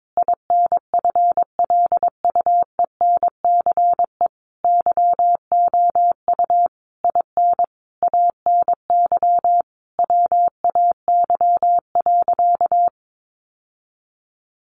Morse code, heard as a keyed tone, about 22 wpm, 700 Hz. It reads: INFLUENCE YOU IN ANY WAY.
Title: Sentences from Top 1100 Words 22wpm